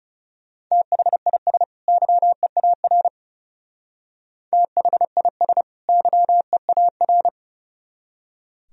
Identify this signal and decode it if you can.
Morse code, heard as a keyed tone, about 37 wpm, 700 Hz. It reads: THIS YEAR THIS YEAR
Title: Speed-Racing - Sets of 2 Words 30wpm